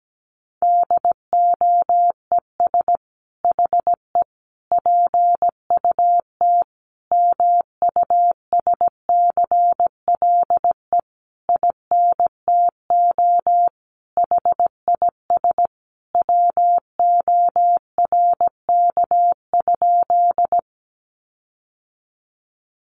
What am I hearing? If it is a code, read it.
Morse code, heard as a keyed tone, about 17 wpm, 700 Hz. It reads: DOES HE PUT MUSCLE INTO HIS WORK?
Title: Sentences from Top 1300 Words 17wpm